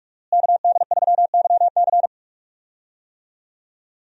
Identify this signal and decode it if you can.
Morse code, heard as a keyed tone, about 47 wpm, 700 Hz. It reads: KD3YC
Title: USA Call Signs - 5 Characters - Repeated 3x - 45wpm